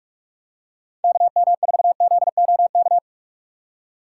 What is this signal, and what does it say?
Morse code, heard as a keyed tone, about 44 wpm, 700 Hz. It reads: KM4ZOK
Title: USA Call Signs - 4x Character Spacing - 45wpm